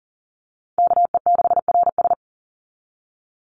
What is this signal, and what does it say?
Morse code, heard as a keyed tone, about 40 wpm, 700 Hz. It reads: KE6RS
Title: NA SOTA Chaser Callsigns 40wpm